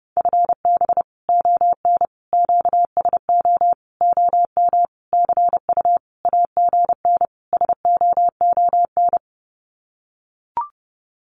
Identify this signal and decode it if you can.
Morse code, heard as a keyed tone, about 28 wpm, 700 Hz. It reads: FB ON QSO OM CU AGN SOON
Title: Common QSO Phrases 30wpm